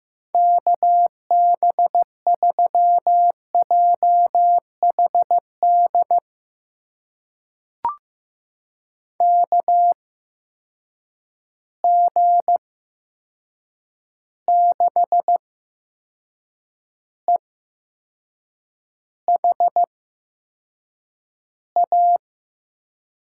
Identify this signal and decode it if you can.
Morse code, heard as a keyed tone, about 15 wpm, spaced as for 6 wpm, 700 Hz. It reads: KB3JHD KG6EHA
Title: USA Call Signs - 6 Characters - 8x Character Spacing - 15wpm